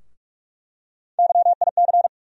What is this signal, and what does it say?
Morse code, heard as a keyed tone, about 42 wpm, 700 Hz. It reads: YIC
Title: USA Call Sign Postfixes - 3 Letters 45wpm